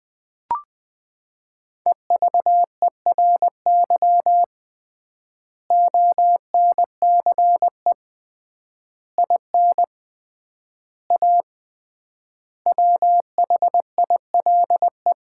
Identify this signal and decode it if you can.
Morse code, heard as a keyed tone, about 20 wpm, 700 Hz. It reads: EVERY ONCE IN A WHILE
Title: 3x Word-Spacing - Sets of 5 Words - 20wpm